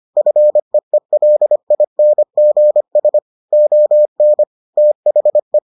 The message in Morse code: FEELINGS ON THE

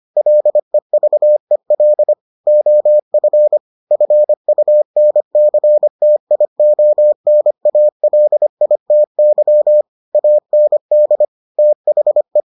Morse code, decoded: LEVEL OF FUNCTIONALITY AND THE

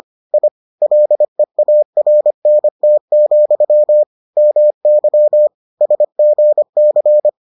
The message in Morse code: I LEARNT, MY SGC

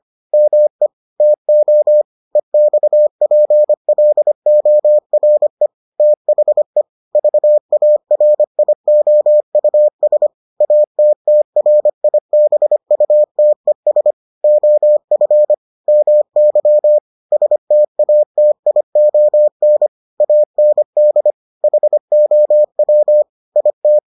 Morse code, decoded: ME TO EXPLORE THE VARIOUS ATTRIBUTES OF MY STATION AND HOW IT

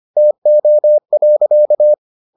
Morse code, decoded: TO.

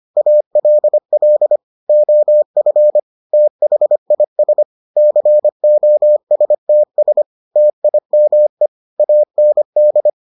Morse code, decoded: ALL OF THIS COSTS TIME AND

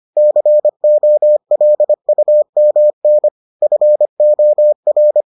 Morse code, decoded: COLUMN FOR

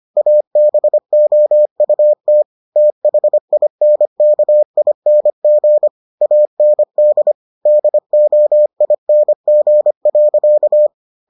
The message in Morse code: ABOUT THINKING AND DOING.

